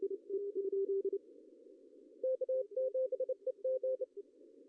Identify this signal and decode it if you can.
Morse code, heard as a keyed tone, about 26 wpm, 520 Hz. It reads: K7EG
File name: CW / cqww-cw-k5tr